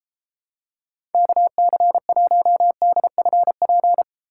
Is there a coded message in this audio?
Yes: Morse code from 1.1 to 4.0 s, heard as a keyed tone, about 33 wpm, 700 Hz. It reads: KC1DFP